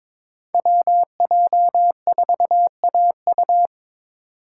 Morse code, heard as a keyed tone, about 22 wpm, 700 Hz. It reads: WJ4AU
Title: Code Groups - 5 Letter:Numbers - 22wpm